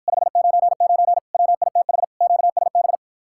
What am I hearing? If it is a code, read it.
Morse code, heard as a keyed tone, about 54 wpm, 700 Hz. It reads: 599 WITH QSB